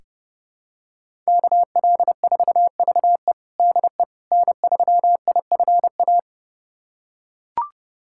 Morse code, heard as a keyed tone, about 32 wpm, 700 Hz. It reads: KL4VE DE N3IFA
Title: USA Callsign DE Callsign - 5 Characters - 30wpm